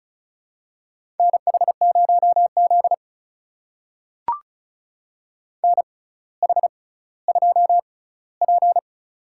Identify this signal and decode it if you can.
Morse code, heard as a keyed tone, about 36 wpm, spaced as for 18 wpm, 700 Hz. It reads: NH0Z NH2P
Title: USA Call Signs - 4 Characters - 6x Character Spacing - 35wpm